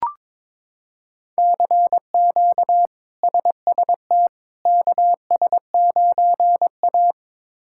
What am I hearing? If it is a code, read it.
Morse code, heard as a keyed tone, about 22 wpm, 700 Hz. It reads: CQ SST KS9A
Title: K1USN SST QSO Practice - 22wpm